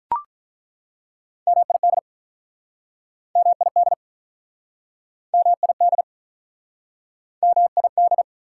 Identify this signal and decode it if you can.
Morse code, heard as a keyed tone, about 48 wpm, 690 Hz. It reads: MID MID MID MID